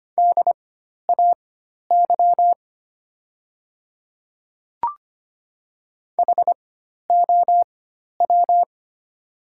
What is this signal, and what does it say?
Morse code, heard as a keyed tone, about 25 wpm, spaced as for 12 wpm, 700 Hz. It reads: DAY HOW
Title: 4x Character Spacing - Top 300 Words - 25wpm